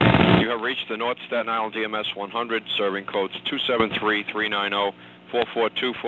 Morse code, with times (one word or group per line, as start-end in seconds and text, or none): none